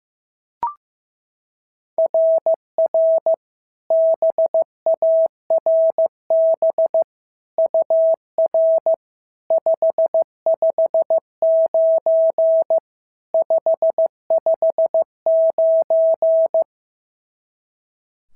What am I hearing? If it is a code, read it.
Morse code, heard as a keyed tone, about 13 wpm, 650 Hz. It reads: RR BARB UR 559 559
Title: SOTA QSO Practice 15wpm